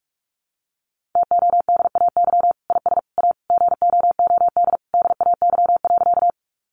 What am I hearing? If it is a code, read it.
Morse code, heard as a keyed tone, about 44 wpm, 700 Hz. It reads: TODAY IS A GOOD DAY.
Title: Sentences from Top 600 Words 45wpm